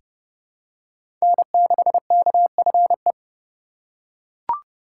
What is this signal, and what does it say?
Morse code, heard as a keyed tone, about 30 wpm, 700 Hz. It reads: N6KFE
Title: USA Call Signs - 5 Characters - 4x Character Spacing - 30wpm